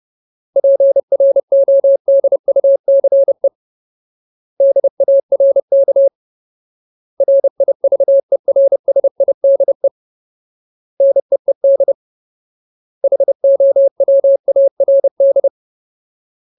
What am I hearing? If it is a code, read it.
Morse code, heard as a keyed tone, about 30 wpm, 550 Hz. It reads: PRODUCE DARK RIVERSIDE NEED HOWARD
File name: Ultimate Word List / 30WPM Ultimate